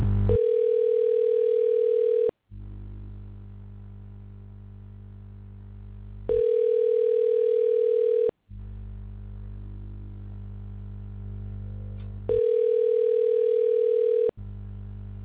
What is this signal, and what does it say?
Ringback tone (North America), 440 + 480 Hz.